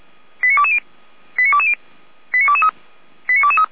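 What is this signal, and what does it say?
Five-tone selective call (ZVEI), heard as quick runs of five tones, tones of about 70 ms. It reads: 89220 89220 89223 89223
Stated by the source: Five-tone ZVEI selective call received on an SDR.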